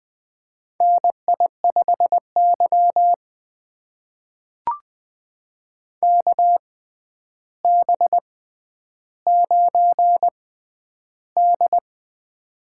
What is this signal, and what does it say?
Morse code, heard as a keyed tone, about 20 wpm, spaced as for 10 wpm, 700 Hz. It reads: NI5Y KB9D
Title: USA Call Signs - 4 Characters - 6x Character Spacing - 20wpm